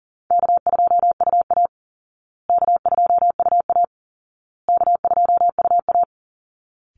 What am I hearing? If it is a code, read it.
Morse code, heard as a keyed tone, about 40 wpm, 700 Hz. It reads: K2UA K2UA K2UA
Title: Call Signs Repeated 2x - 40wpm